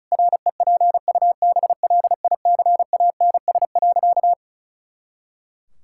Morse code, heard as a keyed tone, about 36 wpm, 700 Hz. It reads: REPUBLICANS.
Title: Pangram Challenge v2 - 35wpm